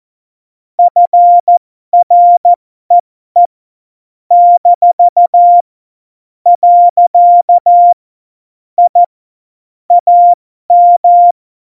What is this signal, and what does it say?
Morse code, heard as a keyed tone, about 14 wpm, spaced as for 12 wpm, 700 Hz. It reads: FREE - . I AM
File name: Morsekurs / Witze